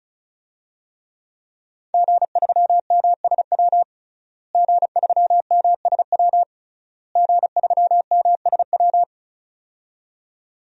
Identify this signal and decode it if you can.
Morse code, heard as a keyed tone, about 36 wpm, 700 Hz. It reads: G3MSW G3MSW G3MSW
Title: DXCC-100 Call Signs - 5 Characters - Repeated 3x - 35wpm